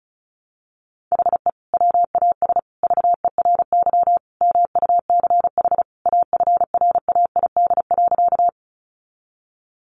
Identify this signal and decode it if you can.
Morse code, heard as a keyed tone, about 34 wpm, 700 Hz. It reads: HE WAS VERY MUCH AFRAID.
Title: Sentences from Top 1500 Words 35wpm